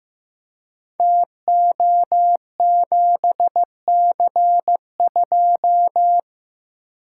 Morse code, heard as a keyed tone, about 15 wpm, 700 Hz. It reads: TO7C2
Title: Code Groups - 5 Letter:Numbers - 15wpm